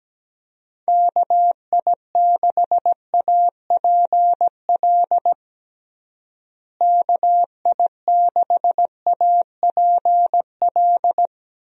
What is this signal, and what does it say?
Morse code, heard as a keyed tone, about 17 wpm, 700 Hz. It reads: KI6APL KI6APL
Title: USA Call Signs - 6 Characters - Repeated 3x - 17wpm